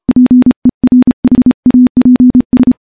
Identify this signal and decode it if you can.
Morse code, heard as a keyed tone, about 28 wpm, 250 Hz. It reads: PERHAPS
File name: morse / morse-training-30WPM